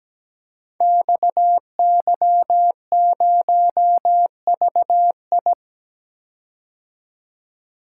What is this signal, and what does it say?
Morse code, heard as a keyed tone, about 17 wpm, 700 Hz. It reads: XY0VI
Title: Code Groups - 5 Letter:Numbers - 17wpm